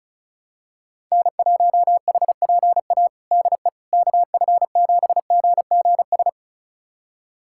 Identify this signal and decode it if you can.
Morse code, heard as a keyed tone, about 34 wpm, 700 Hz. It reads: N1HPA DE KF7GGS
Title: USA Callsign DE Callsign 35wpm